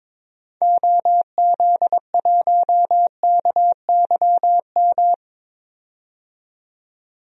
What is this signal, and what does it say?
Morse code, heard as a keyed tone, about 22 wpm, 700 Hz. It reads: OZ1KYM
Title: DXCC-100 Call Signs - 6 Characters - 22wpm